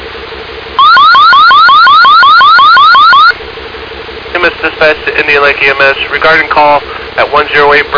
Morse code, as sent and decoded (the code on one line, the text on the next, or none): none